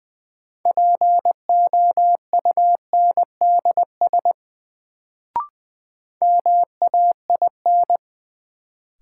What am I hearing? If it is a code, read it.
Morse code, heard as a keyed tone, about 20 wpm, 700 Hz. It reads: POUNDS MAIN
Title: Sentences from Top 1000 Words - Review Part 2 - New Word Drill - 20wpm